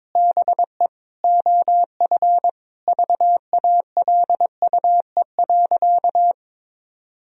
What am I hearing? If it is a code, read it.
Morse code, heard as a keyed tone, about 22 wpm, 700 Hz. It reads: BE OF VALUE.
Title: Sentences from Top 1000 Words - Review Part 3 - 22wpm